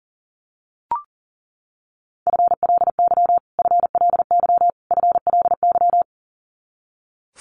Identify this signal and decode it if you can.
Morse code, heard as a keyed tone, about 40 wpm, 700 Hz. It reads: FLY FLY FLY
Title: Top 500 Words Repeated 40wpm